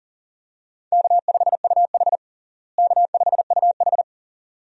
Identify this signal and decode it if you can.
Morse code, heard as a keyed tone, about 37 wpm, 700 Hz. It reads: K5UH K5UH
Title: USA Call Signs - 4 Characters - Repeated 2x - 40wpm